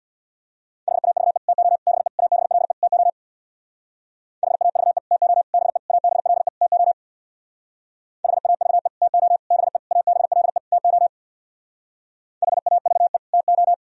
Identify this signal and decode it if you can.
Morse code, heard as a keyed tone, about 64 wpm, 700 Hz. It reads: HAVE TO BE ABLE TO HAVE TO BE ABLE TO HAVE TO BE ABLE TO HAVE TO